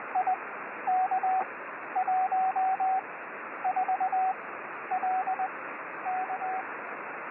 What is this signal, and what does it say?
Morse code, heard as a keyed tone, about 21 wpm, spaced as for 11 wpm, 750 Hz, with radio noise. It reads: IK14LK
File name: 6V7SIX B